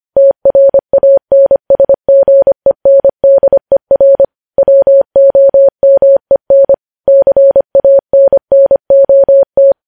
Morse code, heard as a keyed tone, about 25 wpm, 550 Hz. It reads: TRANSGENDER WOMEN CANNOT